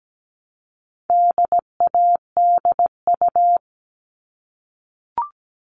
Morse code, heard as a keyed tone, about 17 wpm, 700 Hz. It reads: DADU
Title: All Cut Numbers - 4-Digit 17wpm